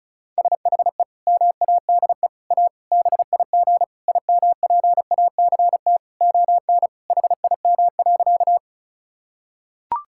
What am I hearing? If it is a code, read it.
Morse code, heard as a keyed tone, about 36 wpm, 700 Hz. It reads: SHE MADE A BIG IMPACT ON HIM.
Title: Sentences from Top 1000 Words - Review Part 6 - 35wpm